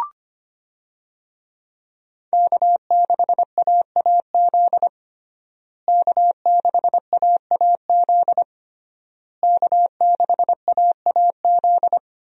Morse code, heard as a keyed tone, about 25 wpm, 700 Hz. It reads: K6AAZ K6AAZ K6AAZ
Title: USA Call Signs - 5 Characters - Repeated 3x - 25wpm